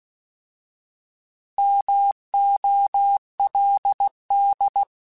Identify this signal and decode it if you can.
Morse code, heard as a keyed tone, about 16 wpm, 800 Hz. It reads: MOLD
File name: Alphabetical / Alphabetical mp3s / Ms @15